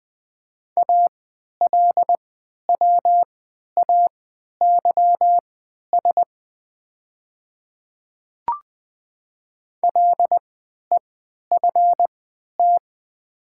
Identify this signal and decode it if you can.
Morse code, heard as a keyed tone, about 20 wpm, spaced as for 11 wpm, 690 Hz. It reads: ALWAYS LEFT
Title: 3x Character Spacing - Top 300 Words - 20wpm